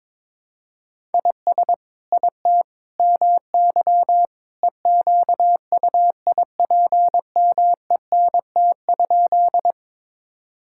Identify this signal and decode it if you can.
Morse code, heard as a keyed tone, about 22 wpm, 700 Hz. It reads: IS IT MY EQUIPMENT?
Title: Sentences from Top 1000 Words - Review Part 5 - 22wpm